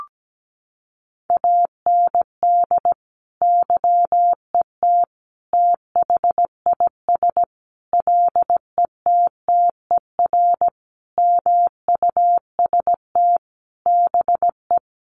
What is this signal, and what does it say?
Morse code, heard as a keyed tone, about 17 wpm, 700 Hz. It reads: AND YET THIS LETTER MUST BE
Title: Sentences from Top 1400 Words 17wpm